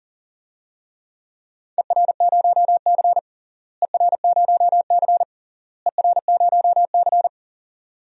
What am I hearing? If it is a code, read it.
Morse code, heard as a keyed tone, about 40 wpm, 700 Hz. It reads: ER0C ER0C ER0C
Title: DXCC-100 Call Signs - 4 Characters - Repeated 3x - 40wpm